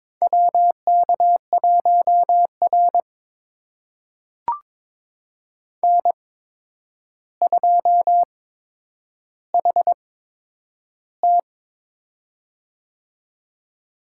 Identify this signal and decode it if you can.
Morse code, heard as a keyed tone, about 22 wpm, spaced as for 9 wpm, 700 Hz. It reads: WK1R N2HT
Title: USA Call Signs - 4 Characters - 8x Character Spacing - 22wpm